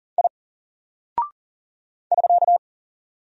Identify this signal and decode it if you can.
Morse code, heard as a keyed tone, about 40 wpm, 700 Hz. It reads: I <SK>
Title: Single Character (Letter:Number:Punctuation:Prosign) Alt 40wpm